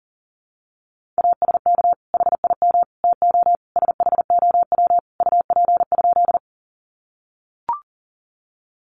Morse code, heard as a keyed tone, about 40 wpm, 700 Hz. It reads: ASK HIM TO SHOW UP?
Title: Sentences from Top 200 Words 40wpm